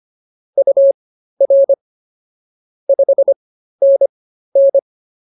Morse code, heard as a keyed tone, about 24 wpm, spaced as for 13 wpm, 550 Hz. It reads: UR 5NN